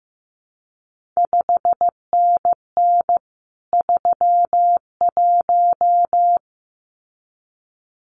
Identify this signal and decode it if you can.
Morse code, heard as a keyed tone, about 15 wpm, 700 Hz. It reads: 5NN 31
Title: CQ WW DX Contest Exchange - 15wpm